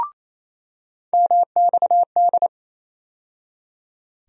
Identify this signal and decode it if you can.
Morse code, heard as a keyed tone, about 28 wpm, 700 Hz. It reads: MXD
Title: USA Call Sign Postfixes - 3 Letters 28wpm